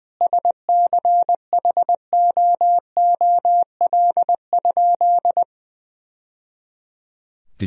Morse code, heard as a keyed tone, about 20 wpm, 700 Hz. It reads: SCHOOL?
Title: Sentences from Top 1000 Words - Review Part 4 - 20wpm